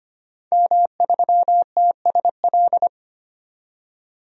Morse code, heard as a keyed tone, about 25 wpm, 700 Hz. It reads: M3TSL
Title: DXCC-100 Call Signs - 5 Characters - 25wpm